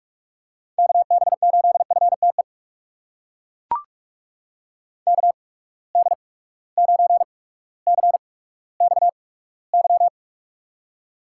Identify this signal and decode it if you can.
Morse code, heard as a keyed tone, about 42 wpm, spaced as for 19 wpm, 700 Hz. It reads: KB8FTE KD9CXY
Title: USA Call Signs - 6 Characters - 8x Character Spacing - 45wpm